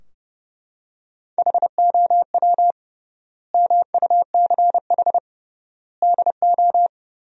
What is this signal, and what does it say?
Morse code, heard as a keyed tone, about 30 wpm, 700 Hz. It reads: HOW MUCH DO
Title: 3x Word-Spacing - Sentences from Top 200 Words - 30wpm